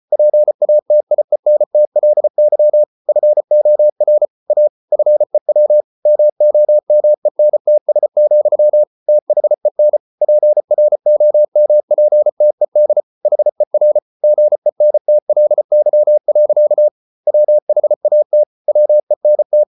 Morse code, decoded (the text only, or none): PATIENTLY FOR A FEW MOMENTS, THEN PROMPTED HER GENTLY. WHAT WENT